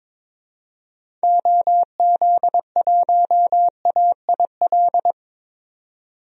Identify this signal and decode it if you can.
Morse code, heard as a keyed tone, about 22 wpm, 700 Hz. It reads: OZ1AIL
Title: DXCC-100 Call Signs - 6 Characters - 22wpm